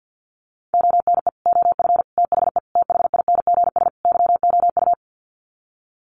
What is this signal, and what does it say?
Morse code, heard as a keyed tone, about 50 wpm, 700 Hz. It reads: ONE OF THE THINGS YOU
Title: Sets of 5 Words Encore2 50wpm